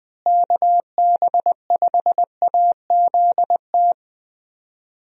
Morse code, heard as a keyed tone, about 20 wpm, 700 Hz. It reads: KB5AZT
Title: USA Call Signs - 6 Characters - Repeated 3x - 20wpm